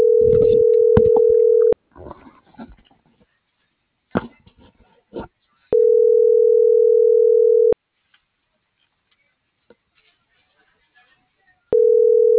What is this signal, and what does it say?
Ringback tone (North America), 440 + 480 Hz.